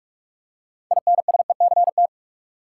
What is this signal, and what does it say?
Morse code, heard as a keyed tone, about 46 wpm, 700 Hz. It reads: INSECT
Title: Six Letter Words - 45wpm